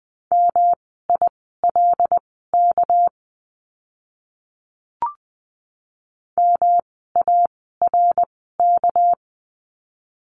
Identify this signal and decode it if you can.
Morse code, heard as a keyed tone, about 20 wpm, spaced as for 14 wpm, 700 Hz. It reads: MILK MARK
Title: 199 Lesson 32f - Letter k - Words 20wpm